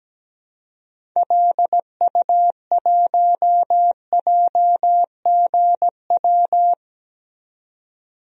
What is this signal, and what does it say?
Morse code, heard as a keyed tone, about 17 wpm, 700 Hz. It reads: LU1JGW
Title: DXCC-100 Call Signs - 6 Characters - 17wpm